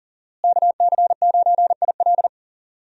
Morse code, heard as a keyed tone, about 40 wpm, 700 Hz. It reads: KC9IL
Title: USA Call Signs - 5 Characters - Repeated 2x - 40wpm